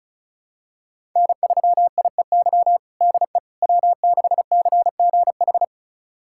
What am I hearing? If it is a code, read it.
Morse code, heard as a keyed tone, about 36 wpm, 700 Hz. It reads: N3IEY DE W6CGH
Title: USA Callsign DE Callsign - 5 Characters - 35wpm